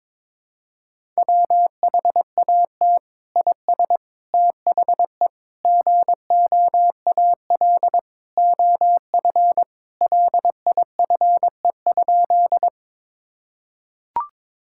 Morse code, heard as a keyed tone, about 22 wpm, 700 Hz. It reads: WHAT IS THE GOAL OF LIFE?
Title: Sentences from Top 1000 Words - Review Part 5 - 22wpm